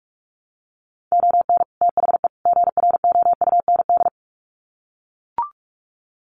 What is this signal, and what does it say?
Morse code, heard as a keyed tone, about 44 wpm, 700 Hz. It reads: ON THE GROUND
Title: Sets of 3 Words 45wpm